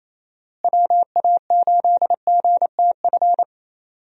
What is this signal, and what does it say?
Morse code, heard as a keyed tone, about 28 wpm, 700 Hz. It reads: WA8GTF
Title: USA Call Signs - 6 Characters - Repeated 3x - 28wpm